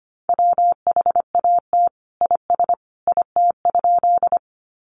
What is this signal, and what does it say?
Morse code, heard as a keyed tone, about 25 wpm, 700 Hz. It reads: WHAT IS IT?
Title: Sentences from Top 100 Words 25wpm